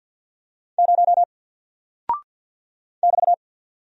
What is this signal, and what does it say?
Morse code, heard as a keyed tone, about 49 wpm, 700 Hz. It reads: 0 <BT>
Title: Single Character (Letter:Number:Punctuation:Prosign) Alt 50wpm